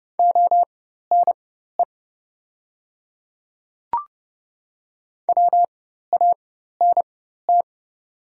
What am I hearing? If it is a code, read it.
Morse code, heard as a keyed tone, about 30 wpm, spaced as for 14 wpm, 700 Hz. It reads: ONE WANT
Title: 4x Character Spacing - Top 200 Words - 30wpm